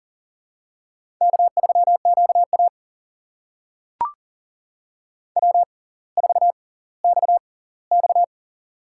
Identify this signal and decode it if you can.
Morse code, heard as a keyed tone, about 40 wpm, spaced as for 20 wpm, 700 Hz. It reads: K3QA W4XX
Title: USA Call Signs - 6x Character Spacing - 40wpm